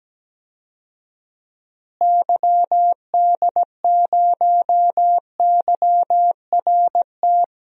Morse code, heard as a keyed tone, about 17 wpm, 700 Hz. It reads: YD0YRT
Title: DXCC-100 Call Signs - 6 Characters - Repeated 2x - 17wpm